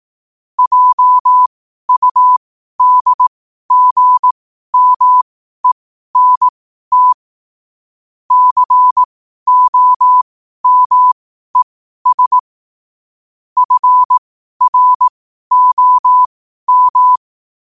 Morse code, heard as a keyed tone, about 18 wpm, spaced as for 12 wpm, 1000 Hz. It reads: JUDGMENT COMES FROM
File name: qotdfiles / 12wpm